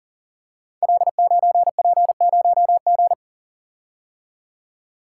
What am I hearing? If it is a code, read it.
Morse code, heard as a keyed tone, about 38 wpm, 700 Hz. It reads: L9P0G
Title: Code Groups - 5 Letter:Numbers - 40wpm